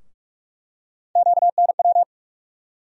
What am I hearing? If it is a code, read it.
Morse code, heard as a keyed tone, about 44 wpm, 700 Hz. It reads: QNW
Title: USA Call Sign Postfixes - 3 Letters 45wpm